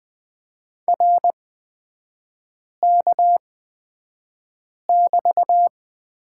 Morse code, heard as a keyed tone, about 20 wpm, spaced as for 5 wpm, 700 Hz. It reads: RK<BT>
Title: 236 Lesson 38e - Letter q - Three Characters 20wpm